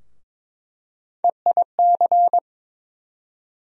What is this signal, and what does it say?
Morse code, heard as a keyed tone, about 22 wpm, 700 Hz. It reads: EIC
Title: USA Call Sign Postfixes - 3 Letters 22wpm